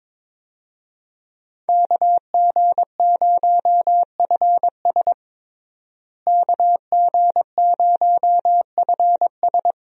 Morse code, heard as a keyed tone, about 22 wpm, 700 Hz. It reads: KG0FS KG0FS
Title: USA Call Signs - 5 Characters - Repeated 2x - 22wpm